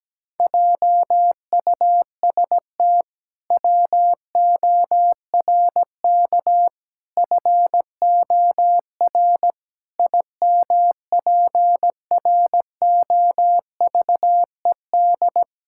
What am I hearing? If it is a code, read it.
Morse code, heard as a keyed tone, about 17 wpm, 700 Hz. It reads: JUST WORK FOR IMPROVED
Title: Pangram Challenge v2 - 17wpm